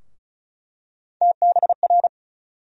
Morse code, heard as a keyed tone, about 34 wpm, 700 Hz. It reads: TBR
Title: USA Call Sign Postfixes - 3 Letters 35wpm